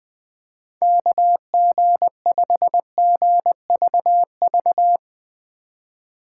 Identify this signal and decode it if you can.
Morse code, heard as a keyed tone, about 21 wpm, 700 Hz. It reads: KG5GVV
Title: Speed Racing - USA 2x3 Call Signs 20wpm